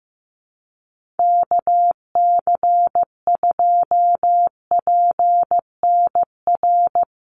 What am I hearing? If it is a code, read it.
Morse code, heard as a keyed tone, about 15 wpm, 700 Hz. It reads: KC2PNR
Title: USA Call Signs 15wpm